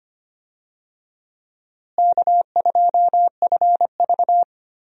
Morse code, heard as a keyed tone, about 25 wpm, 700 Hz. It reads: K2FV
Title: USA Call Signs - 4 Characters - Repeated 3x - 25wpm